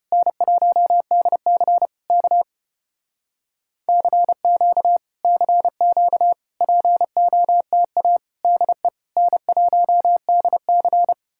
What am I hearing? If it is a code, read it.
Morse code, heard as a keyed tone, about 33 wpm, 700 Hz. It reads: N1DC K CQ CQ POTA DE N1DC